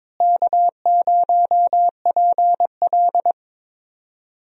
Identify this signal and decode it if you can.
Morse code, heard as a keyed tone, about 22 wpm, 700 Hz. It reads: K0PL
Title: USA Call Signs - 4 Characters - Repeated 3x - 22wpm